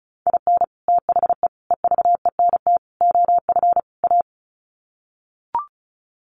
Morse code, heard as a keyed tone, about 34 wpm, 700 Hz. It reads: IN THE EVENT OF A
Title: Sets of 5 Words Encore 35wpm